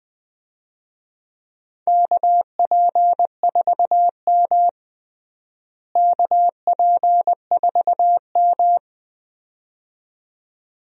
Morse code, heard as a keyed tone, about 21 wpm, 700 Hz. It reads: KP4M KP4M
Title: DXCC-100 Call Signs - 4 Characters - Repeated 2x - 20wpm